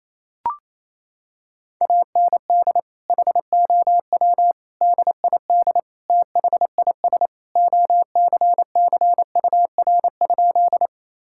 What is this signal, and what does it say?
Morse code, heard as a keyed tone, about 28 wpm, 700 Hz. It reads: AND HOW DID THIS OCCUR?
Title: Sentences from Top 1000 Words - Review Part 6 - 28wpm